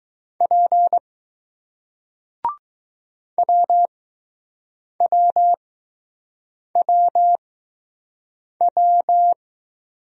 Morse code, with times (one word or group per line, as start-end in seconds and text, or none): 0.4-1.0 P
3.4-9.3 WWWW